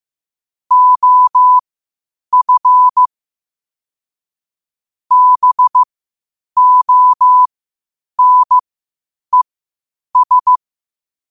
Morse code, heard as a keyed tone, about 15 wpm, spaced as for 8 wpm, 1000 Hz. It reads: OF BONES
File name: qotdfiles / 8wpm